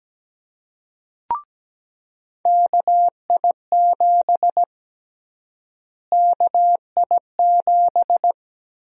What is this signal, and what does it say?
Morse code, heard as a keyed tone, about 17 wpm, 700 Hz. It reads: KI7 KI7
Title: USA Call Sign Prefixes - Repeated 2x - 17wpm